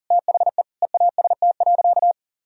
Morse code, heard as a keyed tone, about 40 wpm, 700 Hz. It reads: THE EAST.
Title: Sentences from Top 1000 Words - Review Part 3 - 40wpm